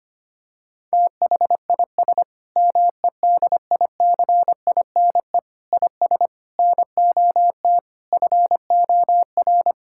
Morse code, heard as a keyed tone, about 25 wpm, 700 Hz. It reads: THIS MEDICINE IS NOT FOR
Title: Sentences from Top 1000 Words - Review Part 6 - 25wpm